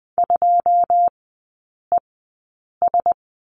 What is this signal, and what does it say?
Morse code, heard as a keyed tone, about 20 wpm, spaced as for 8 wpm, 700 Hz. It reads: 2ES
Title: 081 Lesson 14e - Number 2 - Three Characters 20wpm